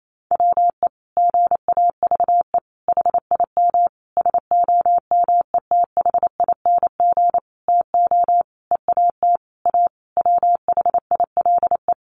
Morse code, heard as a keyed tone, about 28 wpm, 700 Hz. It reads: WE GAVE HIM SOMETHING TO EAT A WHILE
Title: Sentences from Top 600 Words 28wpm